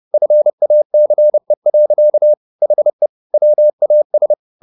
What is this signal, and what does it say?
Morse code, heard as a keyed tone, about 30 wpm, 600 Hz. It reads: FACE. HE WAS